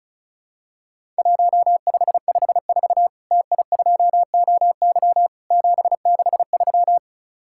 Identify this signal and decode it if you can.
Morse code, heard as a keyed tone, about 36 wpm, 700 Hz. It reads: 1554 TI2OY 763
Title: Worked All Europe DX Contest - QTC Exchange - 35wpm